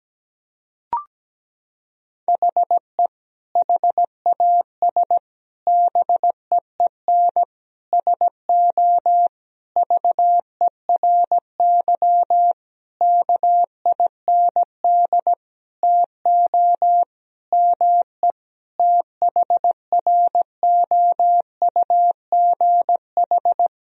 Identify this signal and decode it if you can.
Morse code, heard as a keyed tone, about 17 wpm, 700 Hz. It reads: HE HAS BEEN SO VERY KIND TO ME THROUGH
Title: Sentences from Top 1000 Words - Review Part 1 - 17wpm